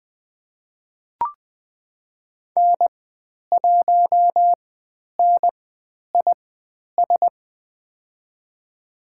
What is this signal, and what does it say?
Morse code, heard as a keyed tone, about 20 wpm, spaced as for 10 wpm, 700 Hz. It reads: N1NIS
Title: 095 Lesson 16g - Letter c - Call Signs 20wpm